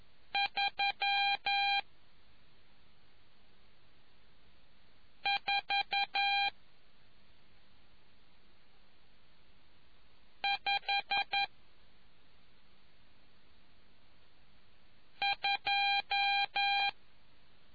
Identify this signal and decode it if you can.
Morse code, heard as a keyed tone, about 11 wpm, 800 Hz. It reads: 3 4 5 2